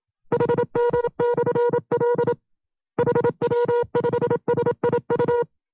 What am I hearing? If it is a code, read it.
Morse code, heard as a keyed tone, about 27 wpm, 480 Hz. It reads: HD/L HW5SIU
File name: FreeFiles / CW-practice / 25WPM-side2